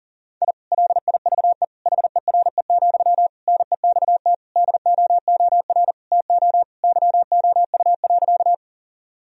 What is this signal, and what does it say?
Morse code, heard as a keyed tone, about 37 wpm, 700 Hz. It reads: I LIVE HERE, NEXT DOOR TO YOU.
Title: Sentences from Top 1000 Words - Review Part 1 - 40wpm